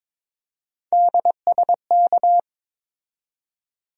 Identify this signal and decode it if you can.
Morse code, heard as a keyed tone, about 22 wpm, 700 Hz. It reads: DSK
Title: USA Call Sign Postfixes - 3 Letters 22wpm